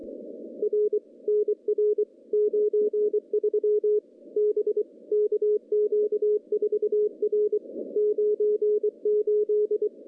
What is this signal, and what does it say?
Morse code, heard as a keyed tone, about 23 wpm, 410 Hz. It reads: R NR 93 B KQ4R 98
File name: contests / CW-SS-k5tr- -0200